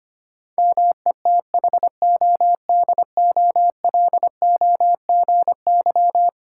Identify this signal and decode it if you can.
Morse code, heard as a keyed tone, about 25 wpm, 700 Hz. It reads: METHODOLOGY
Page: morse-code-ninja-practice-25wpm